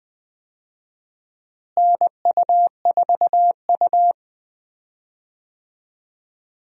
Morse code, heard as a keyed tone, about 20 wpm, 700 Hz. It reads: NU4U
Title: USA Call Signs - 4 Characters - Repeated 3x - 20wpm